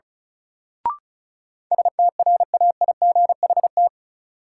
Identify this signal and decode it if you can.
Morse code, heard as a keyed tone, about 34 wpm, 700 Hz. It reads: STRAIGHT
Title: Sentences from Top 1000 Words - New Word Drill - 35wpm